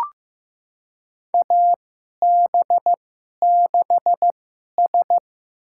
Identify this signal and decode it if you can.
Morse code, heard as a keyed tone, about 15 wpm, spaced as for 11 wpm, 700 Hz. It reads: AB6S
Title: USA Call Signs - 2x Character Spacing - 15wpm